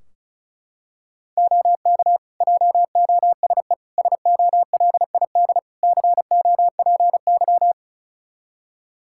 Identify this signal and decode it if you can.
Morse code, heard as a keyed tone, about 36 wpm, 700 Hz. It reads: OK JOSE SOLID COPY
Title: Practice QSO - Exchange 3 - More Info - 35wpm